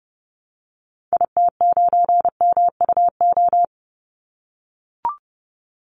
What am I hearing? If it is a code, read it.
Morse code, heard as a keyed tone, about 30 wpm, 700 Hz. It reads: IT9MUO
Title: Call Signs - 30wpm